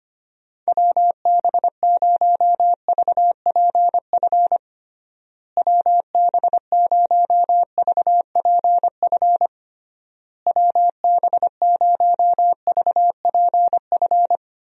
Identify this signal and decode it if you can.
Morse code, heard as a keyed tone, about 25 wpm, 700 Hz. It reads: WB0VPF WB0VPF WB0VPF
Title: USA Call Signs - 6 Characters - Repeated 3x - 25wpm